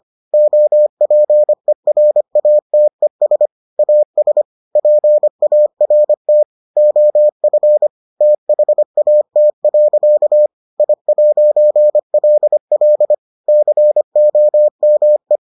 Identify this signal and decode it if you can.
Morse code, heard as a keyed tone, about 25 wpm, 600 Hz. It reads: OPERATES AS PART OF THAT. I'LL COME